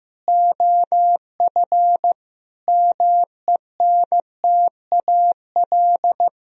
Morse code, heard as a keyed tone, about 15 wpm, 700 Hz. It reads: OF MENTAL
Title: Sentences from Top 1000 Words - Review Part 4 - 15wpm